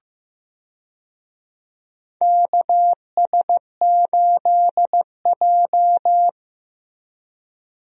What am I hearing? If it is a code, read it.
Morse code, heard as a keyed tone, about 16 wpm, 700 Hz. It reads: KS8J
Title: USA Call Signs - 4 Characters - Repeated 3x - 15wpm